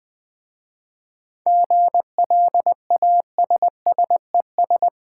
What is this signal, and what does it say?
Morse code, heard as a keyed tone, about 20 wpm, 700 Hz. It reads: GLASSES
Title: Easy Words with Hints - 20wpm